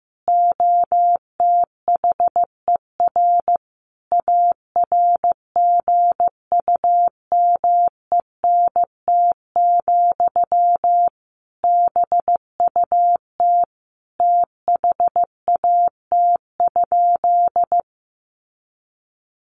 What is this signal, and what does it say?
Morse code, heard as a keyed tone, about 15 wpm, 700 Hz. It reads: OTHER ARGUMENT, BUT THAT?
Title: Sentences from Top 1400 Words 15wpm